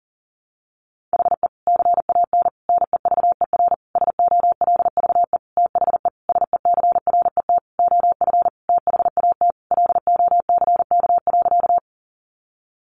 Morse code, heard as a keyed tone, about 40 wpm, 700 Hz. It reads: HE CAN NEVER SOLVE THE SECRET OF THAT LOCK.
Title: Sentences from Top 1300 Words 40wpm